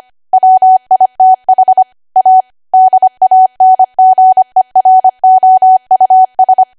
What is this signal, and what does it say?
Morse code, heard as a keyed tone, about 25 wpm, 750 Hz. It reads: WITH A DANGEROUS
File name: Archive / 25 WPM / 25WPM